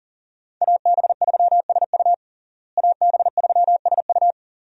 Morse code, heard as a keyed tone, about 40 wpm, 700 Hz. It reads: AB3SU AB3SU
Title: USA Call Signs - 5 Characters - Repeated 3x - 40wpm